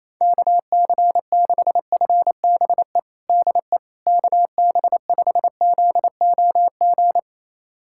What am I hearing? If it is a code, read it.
Morse code, heard as a keyed tone, about 28 wpm, 700 Hz. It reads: KC6FBE DE KB5ZOG
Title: USA Callsign DE Callsign - 6 Characters - 28wpm